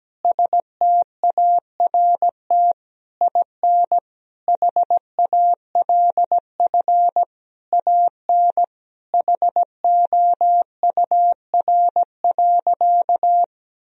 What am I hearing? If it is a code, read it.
Morse code, heard as a keyed tone, about 17 wpm, 700 Hz. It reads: START IN HALF AN HOUR.
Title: Sentences from Top 1000 Words - Review Part 1 - 17wpm